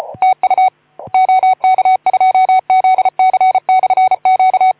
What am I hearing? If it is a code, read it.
Morse code, heard as a keyed tone, about 33 wpm, 750 Hz. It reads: TU OK2ZC/Q
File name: Archiv / CUC181 / OK2ZC CUC 181